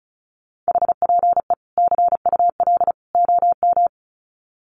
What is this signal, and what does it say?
Morse code, heard as a keyed tone, about 34 wpm, 700 Hz. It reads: HPE CUL OM
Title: Common QSO Phrases 35wpm